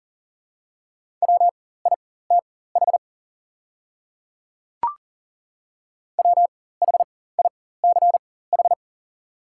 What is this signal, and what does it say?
Morse code, heard as a keyed tone, about 40 wpm, spaced as for 19 wpm, 700 Hz. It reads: WITH WHICH
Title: 4x Character Spacing - Top 100 Words - 40wpm